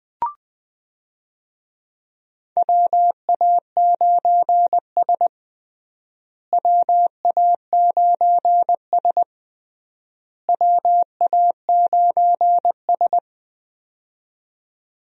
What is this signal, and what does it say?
Morse code, heard as a keyed tone, about 20 wpm, 700 Hz. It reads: WA9S WA9S WA9S
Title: USA Call Signs - 4 Characters - Repeated 3x - 20wpm